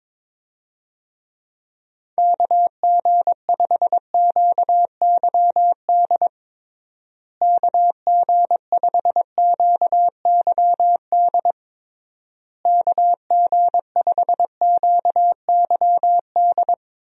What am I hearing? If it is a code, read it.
Morse code, heard as a keyed tone, about 22 wpm, 700 Hz. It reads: KG5QYD KG5QYD KG5QYD
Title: USA Call Signs - 6 Characters - Repeated 3x - 22wpm